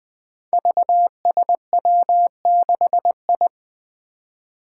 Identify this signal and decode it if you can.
Morse code, heard as a keyed tone, about 20 wpm, 700 Hz. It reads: VSW6I
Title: Five Challenging Letter-Number Combinations 20wpm